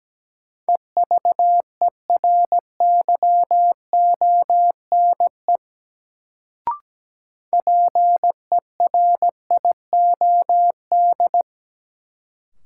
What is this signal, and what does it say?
Morse code, heard as a keyed tone, about 17 wpm, 700 Hz. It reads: EVERYONE PERIOD
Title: Sentences from Top 800 Words - New Word Drill - 17wpm